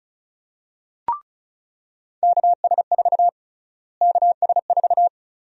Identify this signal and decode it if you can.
Morse code, heard as a keyed tone, about 36 wpm, 700 Hz. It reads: KS4 KS4
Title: USA Call Sign Prefixes - Repeated 2x - 35wpm